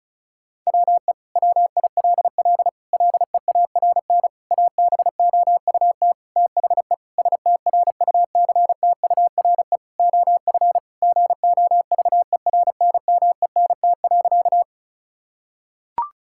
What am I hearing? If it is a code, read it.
Morse code, heard as a keyed tone, about 36 wpm, 700 Hz. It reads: WE WILL LEARN ABOUT THE STRUCTURE OF GOVERNMENT.
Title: Sentences from Top 1000 Words - Review Part 6 - 35wpm